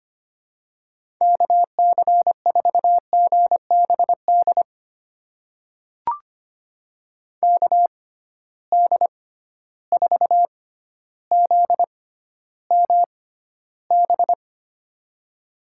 Morse code, heard as a keyed tone, about 25 wpm, spaced as for 13 wpm, 700 Hz. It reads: KC4GBD KD4ZMB
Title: USA Call Signs - 6 Characters - 6x Character Spacing - 25wpm